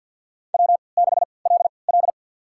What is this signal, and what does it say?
Morse code, heard as a keyed tone, about 50 wpm, spaced as for 28 wpm, 700 Hz. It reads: W6LL
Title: USA Call Signs - 4 Characters - 3x Character Spacing - 50wpm